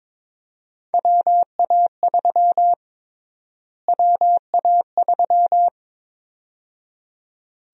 Morse code, heard as a keyed tone, about 23 wpm, 700 Hz. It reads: WA3 WA3
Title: USA Call Sign Prefixes - Repeated 2x - 22wpm